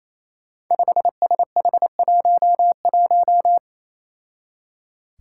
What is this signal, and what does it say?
Morse code, heard as a keyed tone, about 29 wpm, 700 Hz. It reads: 5SH11
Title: Five Challenging Letter-Number Combinations 28wpm